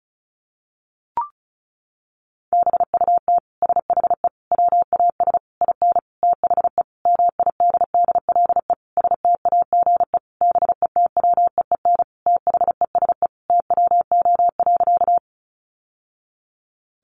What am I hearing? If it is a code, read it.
Morse code, heard as a keyed tone, about 34 wpm, 700 Hz. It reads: BUT SHE WAS IN THE MIDDLE STAGE BETWEEN THESE TWO.
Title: Sentences from Top 1200 Words 35wpm